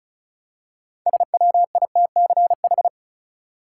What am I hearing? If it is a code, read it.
Morse code, heard as a keyed tone, about 34 wpm, 700 Hz. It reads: SWITCH
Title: Six Letter Words - 35wpm